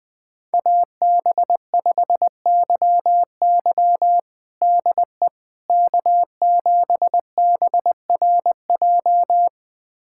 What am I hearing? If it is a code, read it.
Morse code, heard as a keyed tone, about 20 wpm, 700 Hz. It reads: AB5YY DE K7BRJ
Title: USA Callsign DE Callsign - 5 Characters - 20wpm